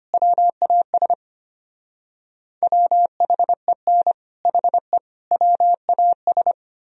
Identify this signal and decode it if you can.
Morse code, heard as a keyed tone, about 32 wpm, 700 Hz. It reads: WAS WHEN HE WAS